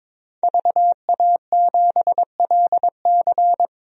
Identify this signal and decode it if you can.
Morse code, heard as a keyed tone, about 22 wpm, 700 Hz. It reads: VA7LC
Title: DXCC-100 Call Signs - 5 Characters - 22wpm